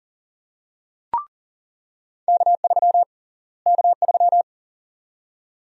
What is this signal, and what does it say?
Morse code, heard as a keyed tone, about 37 wpm, 700 Hz. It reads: K3 K3
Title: USA Call Sign Prefixes - Repeated 2x - 40wpm